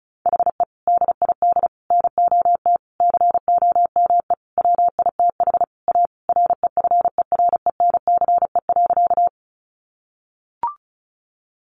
Morse code, heard as a keyed tone, about 34 wpm, 700 Hz. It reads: HE DID NOT COME WITH A REFERENCE.
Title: Sentences from Top 1200 Words 35wpm